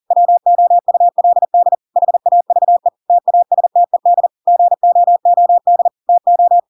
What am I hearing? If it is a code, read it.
Morse code, heard as a keyed tone, about 40 wpm, 700 Hz. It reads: WOULD HAVE TASTED GOOD TO
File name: Pinocchio / Pinocchio ch5 40 40 ik1hge